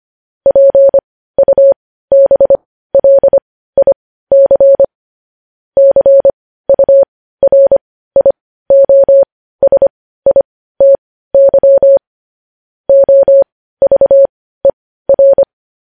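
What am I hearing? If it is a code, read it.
Morse code, heard as a keyed tone, about 25 wpm, spaced as for 15 wpm, 550 Hz. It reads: PUBLIC CURIOSITY OVER
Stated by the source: Morse code news headlines at 15 wpm.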